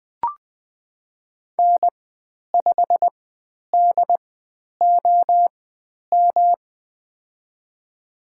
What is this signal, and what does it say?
Morse code, heard as a keyed tone, about 20 wpm, spaced as for 10 wpm, 700 Hz. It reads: N5DOM
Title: 109 Lesson 18g - Letter m - Call Signs 20wpm